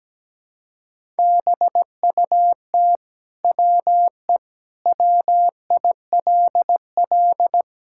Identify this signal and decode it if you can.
Morse code, heard as a keyed tone, about 17 wpm, 700 Hz. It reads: BUT WE WILL
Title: Sentences from Top 1000 Words - Review Part 5 - 17wpm